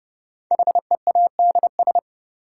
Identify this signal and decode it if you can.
Morse code, heard as a keyed tone, about 31 wpm, 700 Hz. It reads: HEADS
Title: Sentences from Top 1000 Words - Review Part 3 - New Word Drill - 30wpm